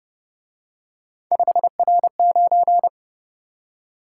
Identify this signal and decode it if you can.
Morse code, heard as a keyed tone, about 27 wpm, 700 Hz. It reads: 5R9
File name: Speed-Racing - Three Letter-Number Combinations - 22wpm